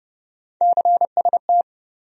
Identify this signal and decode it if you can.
Morse code, heard as a keyed tone, about 30 wpm, 700 Hz. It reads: CST
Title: USA Call Sign Postfixes - 3 Letters 30wpm